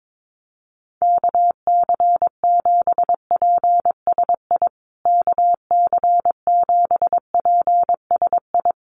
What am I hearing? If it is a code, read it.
Morse code, heard as a keyed tone, about 22 wpm, 700 Hz. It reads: KC7PSI KC7PSI
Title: USA Call Signs Repeated 3x 22wpm